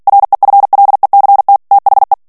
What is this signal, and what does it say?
Morse code, heard as a keyed tone, about 48 wpm, 800 Hz. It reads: REFLECT THE
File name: ANGLAIS texte clair 3 vitesse 45 mots minute